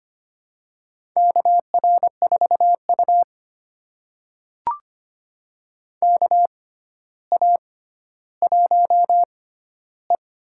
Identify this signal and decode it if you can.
Morse code, heard as a keyed tone, about 25 wpm, spaced as for 13 wpm, 700 Hz. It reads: KR4U KA1E
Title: USA Call Signs - 6x Character Spacing - 25wpm